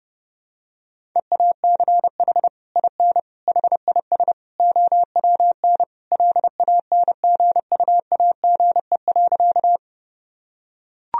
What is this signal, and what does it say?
Morse code, heard as a keyed tone, about 32 wpm, 700 Hz. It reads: EACH IN HIS OWN LANGUAGE.
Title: Sentences from Top 1000 Words - Review Part 3 - 30wpm